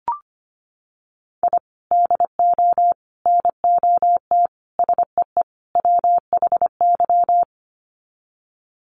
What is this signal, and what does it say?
Morse code, heard as a keyed tone, about 25 wpm, 700 Hz. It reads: I DO NOT SEE WHY
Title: Sets of 5 Words Encore 25wpm